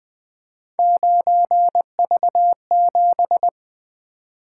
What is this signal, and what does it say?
Morse code, heard as a keyed tone, about 20 wpm, 700 Hz. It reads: 9V7
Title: Three Letter-Number Combinations - 20wpm